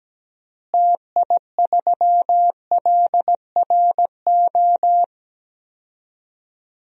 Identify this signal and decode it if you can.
Morse code, heard as a keyed tone, about 17 wpm, 700 Hz. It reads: TI3LRO
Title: DXCC-100 Call Signs - 6 Characters - Repeated 2x - 17wpm